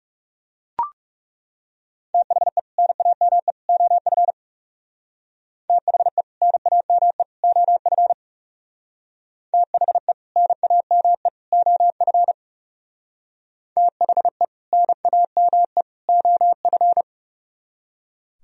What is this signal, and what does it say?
Morse code, heard as a keyed tone, about 37 wpm, 700 Hz. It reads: THE NAME OF THE NAME OF THE NAME OF THE NAME OF
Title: Speed-Racing - Sets of 3 Words 30wpm